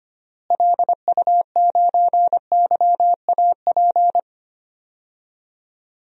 Morse code, heard as a keyed tone, about 25 wpm, 700 Hz. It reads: LU9YAP
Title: DXCC-100 Call Signs - 6 Characters - Repeated 3x - 25wpm